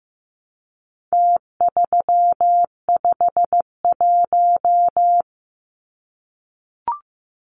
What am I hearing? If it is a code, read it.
Morse code, heard as a keyed tone, about 15 wpm, 700 Hz. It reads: T351
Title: Numbers - Contest Style - 4-Digit 15wpm